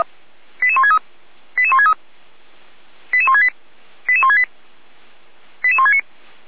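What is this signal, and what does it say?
Five-tone selective call (ZVEI), heard as quick runs of five tones, tones of about 70 ms. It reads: 88163 88163 88168 88168 88169
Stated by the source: ZVEI five-tone selective call received by SDR, on the feuerwehr-orlamuende fire brigade audio server.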